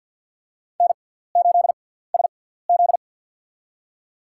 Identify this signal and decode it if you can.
Morse code, heard as a keyed tone, about 53 wpm, spaced as for 17 wpm, 700 Hz. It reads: N8SZ
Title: USA Call Signs - 4 Characters - 6x Character Spacing - 50wpm